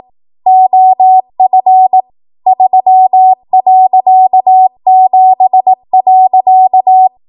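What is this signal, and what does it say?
Morse code, heard as a keyed tone, about 18 wpm, 750 Hz. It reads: OF 3.7.
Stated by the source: ARRL 18 wpm Morse code practice archive transmission.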